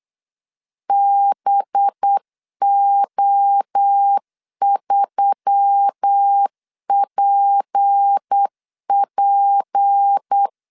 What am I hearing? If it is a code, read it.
Morse code, heard as a keyed tone, about 8 wpm, 800 Hz. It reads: BO3PP
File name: LEZIONI / 6 PKX0 / 6 P 8wpm 3025